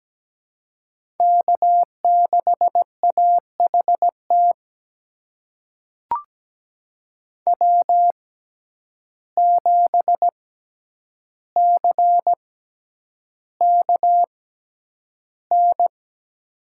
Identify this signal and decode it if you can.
Morse code, heard as a keyed tone, about 17 wpm, spaced as for 9 wpm, 700 Hz. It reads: K6AHT W7CKN
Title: USA Call Signs - 5 Characters - 6x Character Spacing - 17wpm